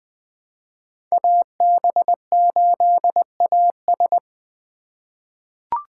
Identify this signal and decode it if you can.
Morse code, heard as a keyed tone, about 20 wpm, 700 Hz. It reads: AB8AS
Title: 2023 POTA Activator Call Signs - 20wpm